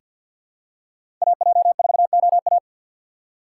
Morse code, heard as a keyed tone, about 50 wpm, 700 Hz. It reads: AJ4OA
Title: USA Call Signs - 5 Characters - 3x Character Spacing - 50wpm